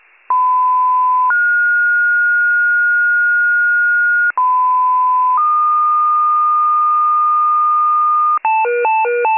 Two-tone page, heard as a one-second tone, then a three-second tone, about 1010 then 1430 Hz, about 1010 then 1220 Hz, with noise. It is heard from 0.3 to 8.4 s.